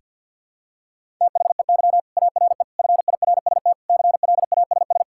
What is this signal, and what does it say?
Morse code, heard as a keyed tone, about 51 wpm, 700 Hz. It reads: THEY ARE FIRST CLASS
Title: Sentences from Top 1000 Words - Review Part 5 - 50wpm